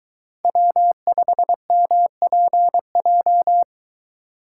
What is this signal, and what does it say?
Morse code, heard as a keyed tone, about 22 wpm, 700 Hz. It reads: W5MPJ